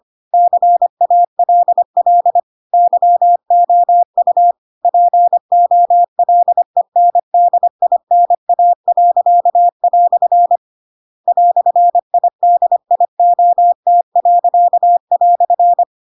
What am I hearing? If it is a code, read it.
Morse code, heard as a keyed tone, about 25 wpm, 700 Hz. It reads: CALL YOU POLENDINA." "IDIOT."
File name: Pinocchio / Pinocchio ch2 25 25 ik1hge